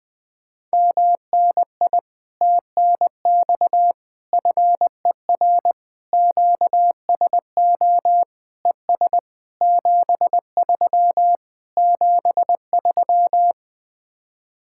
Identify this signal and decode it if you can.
Morse code, heard as a keyed tone, about 20 wpm, 700 Hz. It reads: MNI TNX FER QSO ES 73 73
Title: Speed-Racing Common QSO Phrases - 20wpm